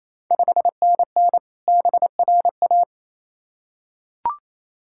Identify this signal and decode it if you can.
Morse code, heard as a keyed tone, about 28 wpm, 700 Hz. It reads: 5NN BRA
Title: Ontario QSO Party Exchange 28wpm